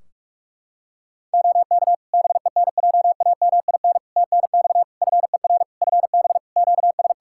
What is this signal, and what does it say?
Morse code, heard as a keyed tone, about 46 wpm, 700 Hz. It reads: OK BENJAMIN TNX FER FB QS
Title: K1USN SST QSO Practice 45wpm